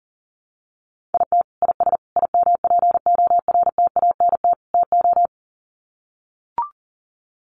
Morse code, heard as a keyed tone, about 40 wpm, 700 Hz. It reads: IT IS IMPORTANT TO
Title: Sets of 4 Words 40wpm